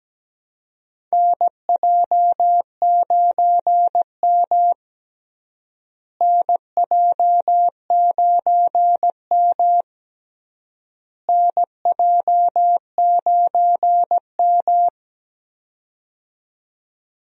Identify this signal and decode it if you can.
Morse code, heard as a keyed tone, about 17 wpm, 700 Hz. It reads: NJ9M NJ9M NJ9M
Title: USA Call Signs - 4 Characters - Repeated 3x - 17wpm